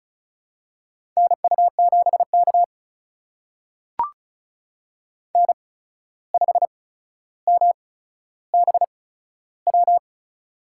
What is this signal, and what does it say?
Morse code, heard as a keyed tone, about 34 wpm, spaced as for 14 wpm, 700 Hz. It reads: NU7K N5MBW
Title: USA Call Signs - 8x Character Spacing - 35wpm